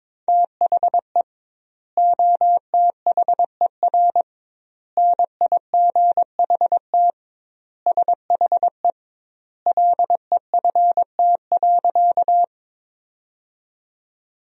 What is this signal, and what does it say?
Morse code, heard as a keyed tone, about 21 wpm, 700 Hz. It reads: THE OTHER NIGHT SHE LEFT.
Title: 2x Word-Spacing - Sentences from Top 300 Words - 22wpm